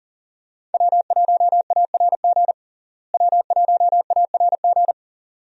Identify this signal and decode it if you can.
Morse code, heard as a keyed tone, about 40 wpm, 700 Hz. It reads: W1ARG W1ARG
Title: USA Call Signs - 5 Characters - Repeated 2x - 40wpm